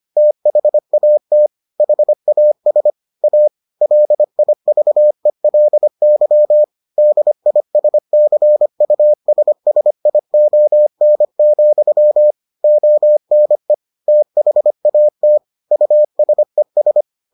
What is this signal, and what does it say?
Morse code, heard as a keyed tone, about 25 wpm, 600 Hz. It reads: THAT HAS A LIVELY DISCUSSION, ONE THAT USES